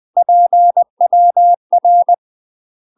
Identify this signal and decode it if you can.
Morse code, heard as a keyed tone, about 20 wpm, 700 Hz. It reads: PWR
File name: UK MASTER QSOs TRAINING / 20WPM / UK MASTER QSOs 0047